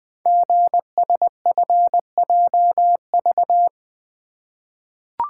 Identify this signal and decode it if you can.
Morse code, heard as a keyed tone, about 20 wpm, 700 Hz. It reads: GSFJV
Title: Code Groups - 5 Letters - 20wpm